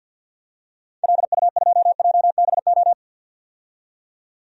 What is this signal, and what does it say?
Morse code, heard as a keyed tone, about 49 wpm, 700 Hz. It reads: LU2JBO
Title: DXCC-100 Call Signs - 6 Characters - 50wpm